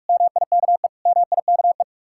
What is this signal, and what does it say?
Morse code, heard as a keyed tone, about 56 wpm, 700 Hz. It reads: MIKE MIKE
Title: Speed-Racing Common QSO Phrases - 45wpm